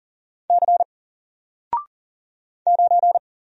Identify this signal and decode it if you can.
Morse code, heard as a keyed tone, about 39 wpm, 700 Hz. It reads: C 9
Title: Single Character (Letter:Number:Punctuation:Prosign) 40wpm